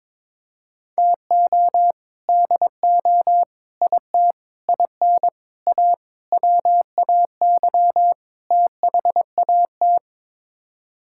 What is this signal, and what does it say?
Morse code, heard as a keyed tone, about 21 wpm, 700 Hz. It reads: TO DO IT IN A WAY THAT
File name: practice / 22wpm / Sets of 7 Words - 22wpm